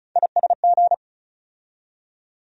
Morse code, heard as a keyed tone, about 34 wpm, 700 Hz. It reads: ISG
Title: USA Call Sign Postfixes - 3 Letters 35wpm